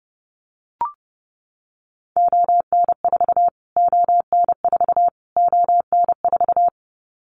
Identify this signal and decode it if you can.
Morse code, heard as a keyed tone, about 30 wpm, 700 Hz. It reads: ON4 ON4 ON4
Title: Callsign Prefixes for Top 25 DXCC - Repeated 3x - 30wpm